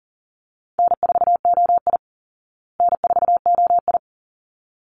Morse code, heard as a keyed tone, about 40 wpm, 700 Hz. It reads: N4OI N4OI
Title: Easy Call Signs Repeated 2x - 40wpm